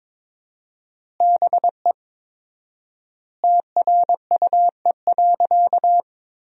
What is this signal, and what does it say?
Morse code, heard as a keyed tone, about 22 wpm, 700 Hz. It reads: BE TRUE.
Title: 4x Word-Spacing - Sentences from Top 400 Words - 22wpm